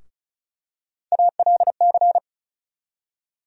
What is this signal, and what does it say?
Morse code, heard as a keyed tone, about 36 wpm, 700 Hz. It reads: ALC
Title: USA Call Sign Postfixes - 3 Letters 35wpm